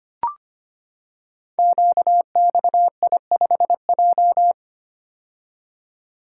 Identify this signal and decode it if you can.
Morse code, heard as a keyed tone, about 25 wpm, 700 Hz. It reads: QXI5J
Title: Code Groups - 5 Letter:Numbers - 25wpm